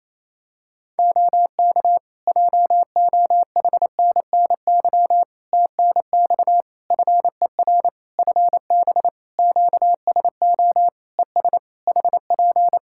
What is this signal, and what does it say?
Morse code, heard as a keyed tone, about 28 wpm, 700 Hz. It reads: OK JOHNNY TNX FER FB QSO ES HP
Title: Practice QSO - Exchange 4 - Ending - 28wpm